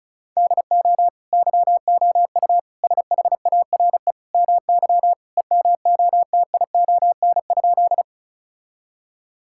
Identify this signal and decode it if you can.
Morse code, heard as a keyed tone, about 36 wpm, 700 Hz. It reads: DO YOU SHARE MY EMOTION?
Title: Sentences from Top 1600 Words 35wpm